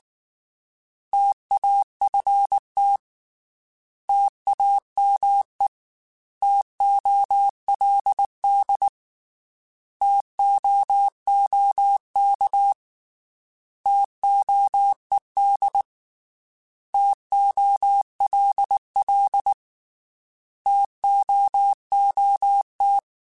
Morse code, heard as a keyed tone, about 19 wpm, 800 Hz. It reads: TAFT TAME TOLD TOOK TOED TOLL TOOT
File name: T Prefixes / T mp3s